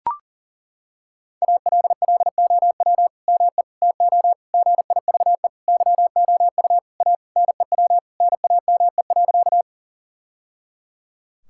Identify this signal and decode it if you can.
Morse code, heard as a keyed tone, about 37 wpm, 700 Hz. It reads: ALLOW ME TO GIVE YOU A NEW NAME.
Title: Sentences from Top 1000 Words - Review Part 5 - 40wpm